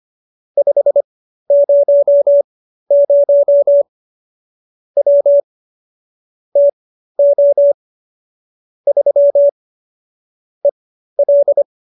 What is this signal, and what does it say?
Morse code, heard as a keyed tone, about 25 wpm, spaced as for 13 wpm, 570 Hz. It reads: 500 W TO 3 EL